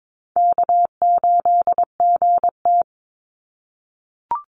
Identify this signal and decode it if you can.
Morse code, heard as a keyed tone, about 22 wpm, 700 Hz. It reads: K8GT
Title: Call Signs - 22wpm